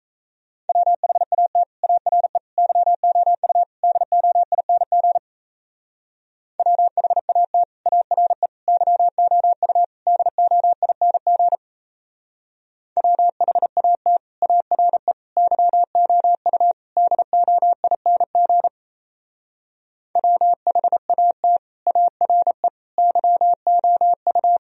Morse code, heard as a keyed tone, about 36 wpm, 700 Hz. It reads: WHAT ARE YOU DOING WHAT ARE YOU DOING WHAT ARE YOU DOING WHAT ARE YOU